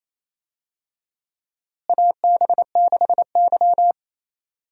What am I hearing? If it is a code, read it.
Morse code, heard as a keyed tone, about 27 wpm, 700 Hz. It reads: AB6Y
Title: USA Call Signs - 4 Characters - Repeated 2x - 28wpm